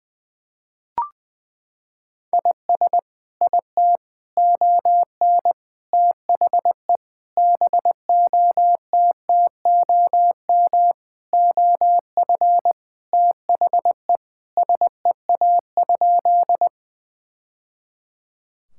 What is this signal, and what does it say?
Morse code, heard as a keyed tone, about 20 wpm, 700 Hz. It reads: IS IT ON THE BOTTOM OF THE SEA?
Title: Sentences from Top 1000 Words - Review Part 6 - 20wpm